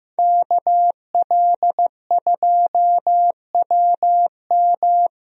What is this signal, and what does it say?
Morse code, heard as a keyed tone, about 15 wpm, 700 Hz. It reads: KL2WM
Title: USA Call Signs - 5 Characters - Repeated 3x - 15wpm